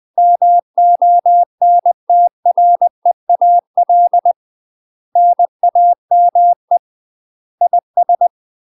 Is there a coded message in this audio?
Yes: Morse code from 0.2 to 8.3 s, heard as a keyed tone, about 20 wpm, 700 Hz. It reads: MONTREAL NAME IS